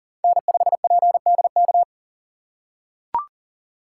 Morse code, heard as a keyed tone, about 40 wpm, 700 Hz. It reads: N5PDK
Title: USA Call Signs - 5 Characters - 4x Character Spacing - 40wpm